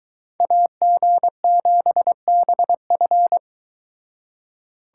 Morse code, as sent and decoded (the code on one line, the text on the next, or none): .- --. --... -... ..-.
AG7BF